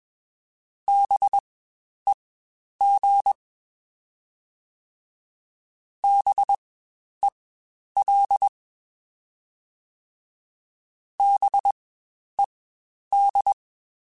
Morse code, heard as a keyed tone, about 21 wpm, spaced as for 10 wpm, 800 Hz. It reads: BEG BEL BED
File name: Spaced Alphas / Spaced Alphas mp3s / Bs sp @20